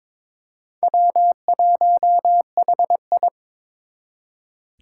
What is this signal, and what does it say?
Morse code, heard as a keyed tone, about 23 wpm, 700 Hz. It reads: W1HI
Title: Four Challenging Letter-Number Combinations 22wpm